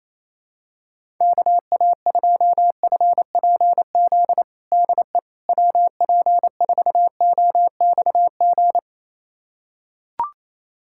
Morse code, heard as a keyed tone, about 28 wpm, 700 Hz. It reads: KA2FPZ DE WP4OXG
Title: USA Callsign DE Callsign - 6 Characters - 28wpm